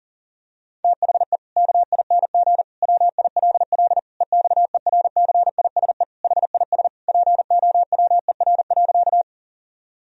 Morse code, heard as a keyed tone, about 40 wpm, 700 Hz. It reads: THE KING WILL EXERCISE HIS POWER.
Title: Sentences from Top 1000 Words - Review Part 4 - 40wpm